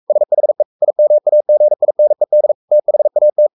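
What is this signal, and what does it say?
Morse code, heard as a keyed tone, about 44 wpm, 600 Hz. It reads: SHE IMAGINED THAT
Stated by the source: Morse code at 43 wpm sending Ken Follett's novel Never in English, from CW med Gnister.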